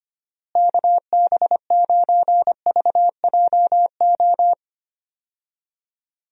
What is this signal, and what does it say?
Morse code, heard as a keyed tone, about 25 wpm, 700 Hz. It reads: KB9VJO
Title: USA Call Signs - 6 Characters - Repeated 3x - 25wpm